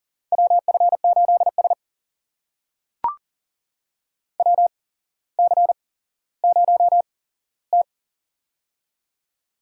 Morse code, heard as a keyed tone, about 42 wpm, spaced as for 17 wpm, 700 Hz. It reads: WF8S WC0T
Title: USA Call Signs - 4 Characters - 8x Character Spacing - 40wpm